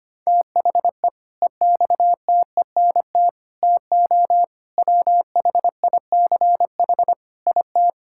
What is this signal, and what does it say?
Morse code, heard as a keyed tone, about 25 wpm, 700 Hz. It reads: THE EXTENT TO WHICH IT
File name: practice / 25wpm / Sentences with 4-Word Phrases - 25wpm